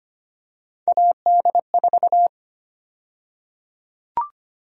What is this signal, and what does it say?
Morse code, heard as a keyed tone, about 25 wpm, 700 Hz. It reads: AD4
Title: USA Call Sign Prefixes - Repeated 2x - 25wpm